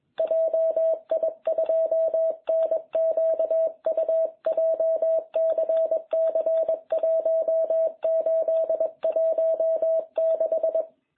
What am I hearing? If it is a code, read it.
Morse code, heard as a keyed tone, about 21 wpm, 660 Hz. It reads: JI2NQU JCC 1816